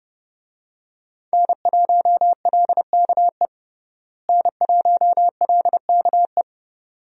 Morse code, heard as a keyed tone, about 32 wpm, 700 Hz. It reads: N1LKE N1LKE
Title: USA Call Signs - 5 Characters - Repeated 2x - 30wpm